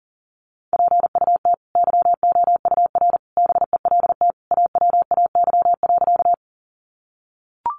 Morse code, heard as a keyed tone, about 40 wpm, 700 Hz. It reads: PUT YOUR BELT AWAY.
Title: Sentences from Top 1400 Words 40wpm